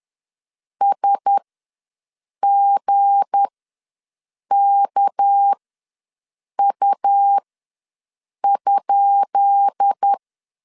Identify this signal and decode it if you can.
Morse code, heard as a keyed tone, about 10 wpm, spaced as for 6 wpm, 800 Hz. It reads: SGKU?
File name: LEZIONI / 10 SIGNS / 10B INT ws 10wpm 4134